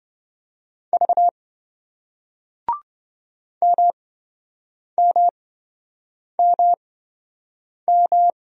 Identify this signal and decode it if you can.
Morse code, heard as a keyed tone, about 27 wpm, spaced as for 7 wpm, 700 Hz. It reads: V MMMM